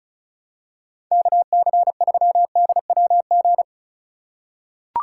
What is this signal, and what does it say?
Morse code, heard as a keyed tone, about 36 wpm, 700 Hz. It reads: KC3DWG
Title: USA Call Signs - 4x Character Spacing - 35wpm